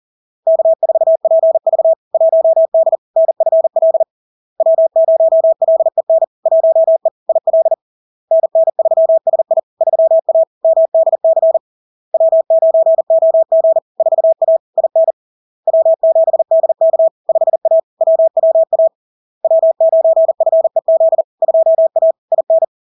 Morse code, heard as a keyed tone, about 40 wpm, 650 Hz. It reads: K4PV 1D NFL W0LEN 1E IL NN3SI 3A MDC W9OG 4A IN W7DK 5A WWA W9FEZ 2A IN
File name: LARK-FD-2022-Log-in-CW-40wpm-650-Hz